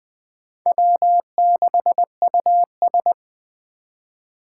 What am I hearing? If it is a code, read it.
Morse code, heard as a keyed tone, about 20 wpm, 700 Hz. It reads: W6US